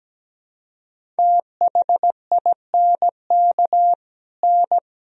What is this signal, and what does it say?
Morse code, heard as a keyed tone, about 17 wpm, 700 Hz. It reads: THINK N
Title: Sentences from Top 200 Words and Abbreviations 17wpm